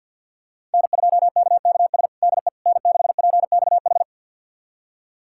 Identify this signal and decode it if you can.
Morse code, heard as a keyed tone, about 46 wpm, 700 Hz. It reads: N2KKS DE N6PXH
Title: USA Callsign DE Callsign - 5 Characters - 50wpm